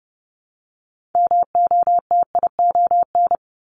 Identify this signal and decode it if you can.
Morse code, heard as a keyed tone, about 30 wpm, 700 Hz. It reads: MOTION
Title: Multi-letter Phonemes Repeated 3x 30wpm